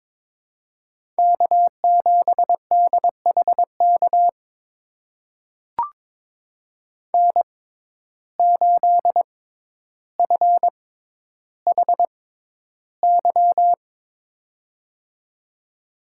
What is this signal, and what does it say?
Morse code, heard as a keyed tone, about 22 wpm, spaced as for 11 wpm, 700 Hz. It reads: K7DHK N8FHY
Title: USA Call Signs - 5 Characters - 6x Character Spacing - 22wpm